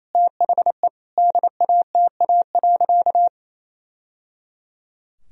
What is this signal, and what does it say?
Morse code, heard as a keyed tone, about 28 wpm, 700 Hz. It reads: THE DATA.
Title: Sentences from Top 1000 Words - Review Part 3 - 28wpm